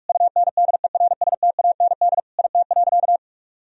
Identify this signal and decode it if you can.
Morse code, heard as a keyed tone, about 47 wpm, 700 Hz. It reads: UNDERSTAND IT.
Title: Sentences from Top 1000 Words - Review Part 4 - 45wpm